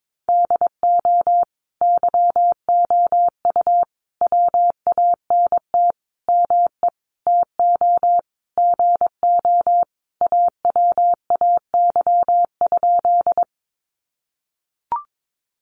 Morse code, heard as a keyed tone, about 22 wpm, 700 Hz. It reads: DO YOU WANT ME TO GO AWAY?
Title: Sentences from Top 600 Words 22wpm